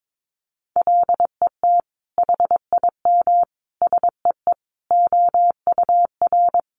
Morse code, heard as a keyed tone, about 22 wpm, 700 Hz. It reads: LET HIM SEE OUR
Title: Sentences from Top 1400 Words 22wpm